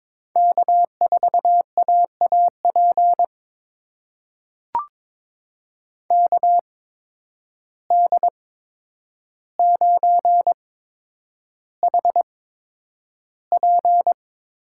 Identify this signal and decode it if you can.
Morse code, heard as a keyed tone, about 22 wpm, spaced as for 9 wpm, 700 Hz. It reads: K4AAP KD9HP
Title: USA Call Signs - 5 Characters - 8x Character Spacing - 22wpm